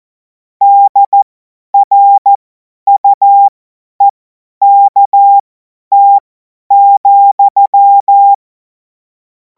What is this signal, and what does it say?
Morse code, heard as a keyed tone, about 14 wpm, spaced as for 10 wpm, 800 Hz. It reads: DRUEKT,
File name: Morsekurs / Witze / Witz 18. August 2026 50 F70